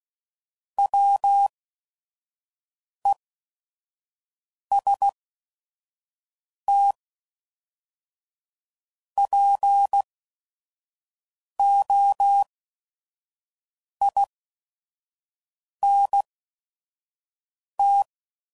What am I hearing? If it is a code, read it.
Morse code, heard as a keyed tone, about 16 wpm, spaced as for 5 wpm, 800 Hz. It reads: WESTPOINT
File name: CW Code Groups @5WPM / fromSM's / Cities